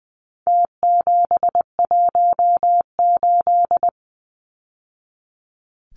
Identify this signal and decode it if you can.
Morse code, heard as a keyed tone, about 20 wpm, 700 Hz. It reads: T718
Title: Numbers - Contest Style - 4-Digit 20wpm